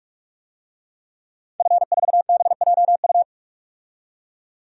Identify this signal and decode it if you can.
Morse code, heard as a keyed tone, about 43 wpm, 700 Hz. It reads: F4BJU
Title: DXCC-100 Call Signs - 5 Characters - 45wpm